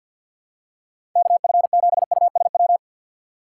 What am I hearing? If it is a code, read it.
Morse code, heard as a keyed tone, about 51 wpm, 700 Hz. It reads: KF7USW
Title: USA Call Signs - 6 Characters - 3x Character Spacing - 50wpm